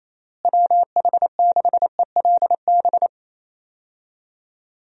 Morse code, heard as a keyed tone, about 28 wpm, 700 Hz. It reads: WH6ELB
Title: USA Call Signs - 6 Characters - Repeated 3x - 28wpm